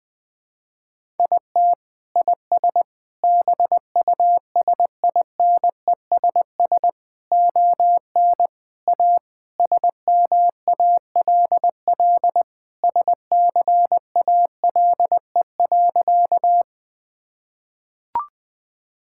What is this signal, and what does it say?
Morse code, heard as a keyed tone, about 20 wpm, 700 Hz. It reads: IT IS BUSINESS ON A SMALL SCALE.
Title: Sentences from Top 1000 Words - Review Part 5 - 20wpm